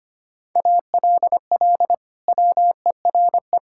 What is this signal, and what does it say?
Morse code, heard as a keyed tone, about 25 wpm, 700 Hz. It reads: ALL WERE
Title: Sentences from Top 1000 Words - Review Part 5 - 25wpm